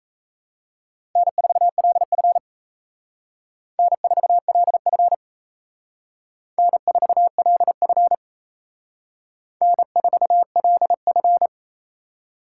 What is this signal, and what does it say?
Morse code, heard as a keyed tone, about 35 wpm, 700 Hz. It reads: N4LF N4LF N4LF N4LF
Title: Speed Racing - USA 1x2 Call Signs 28wpm